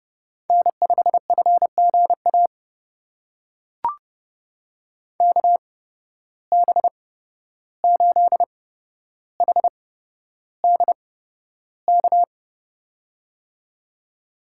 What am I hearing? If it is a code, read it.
Morse code, heard as a keyed tone, about 30 wpm, spaced as for 12 wpm, 700 Hz. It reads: N5FGA KB8HDK
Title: USA Call Signs - 8x Character Spacing - 30wpm